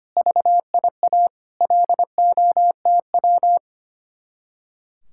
Morse code, heard as a keyed tone, about 31 wpm, 700 Hz. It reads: VIA LOTW
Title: Speed-Racing Common QSO Phrases - 25wpm